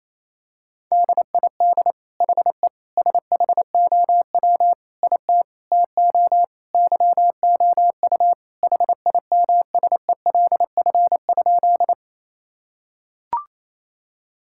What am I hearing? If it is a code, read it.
Morse code, heard as a keyed tone, about 28 wpm, 700 Hz. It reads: DID HE SHOW IT TO YOU HIMSELF?
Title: Sentences from Top 1600 Words 28wpm